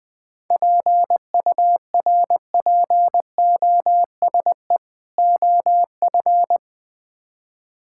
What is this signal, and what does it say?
Morse code, heard as a keyed tone, about 20 wpm, 700 Hz. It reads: PURPOSE OF
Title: Build A Phrase - 5-Word Phrases - 20wpm